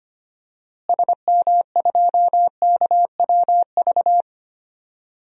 Morse code, heard as a keyed tone, about 25 wpm, 700 Hz. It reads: SM2KWV
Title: DXCC-100 Call Signs - 6 Characters - Repeated 2x - 25wpm